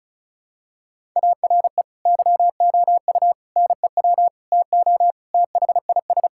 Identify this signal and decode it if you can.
Morse code, heard as a keyed tone, about 36 wpm, 700 Hz. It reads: ARE YOU NEW TO THIS
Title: Sentences from Top 1000 Words - Review Part 6 - 35wpm